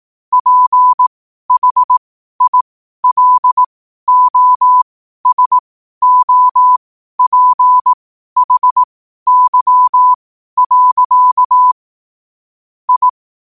Morse code, heard as a keyed tone, about 18 wpm, spaced as for 12 wpm, 1000 Hz. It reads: PHILOSOPHY. I
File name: qotdfiles / 12wpm